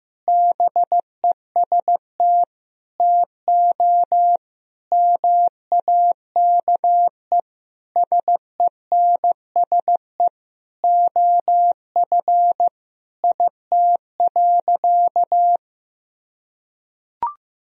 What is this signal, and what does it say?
Morse code, heard as a keyed tone, about 16 wpm, 700 Hz. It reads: BEST TO MAKE SENSE OF IT.
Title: Sentences from Top 1000 Words - Review Part 4 - 15wpm